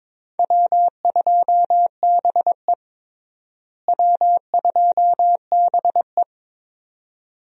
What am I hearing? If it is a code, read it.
Morse code, heard as a keyed tone, about 22 wpm, 700 Hz. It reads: W2BE W2BE
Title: USA Call Signs - 4 Characters - Repeated 2x - 22wpm